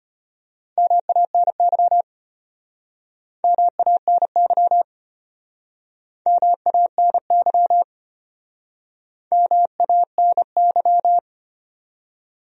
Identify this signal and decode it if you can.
Morse code, heard as a keyed tone, about 33 wpm, 700 Hz. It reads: MANY MANY MANY MANY